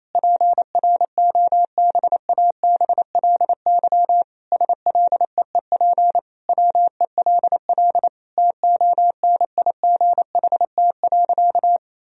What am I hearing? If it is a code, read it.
Morse code, heard as a keyed tone, about 28 wpm, 700 Hz. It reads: PROBABLY SLEEP WELL TONIGHT.
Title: Sentences from Top 1000 Words - Review Part 4 - 28wpm